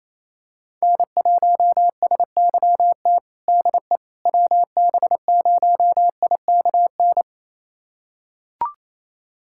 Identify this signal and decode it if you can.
Morse code, heard as a keyed tone, about 28 wpm, 700 Hz. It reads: N1SYT DE WB0IKN
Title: USA Callsign DE Callsign 28wpm